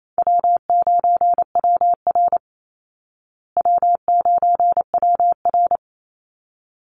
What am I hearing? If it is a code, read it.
Morse code, heard as a keyed tone, about 28 wpm, 700 Hz. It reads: W9WR W9WR
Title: Call Signs Repeated 3x - 28wpm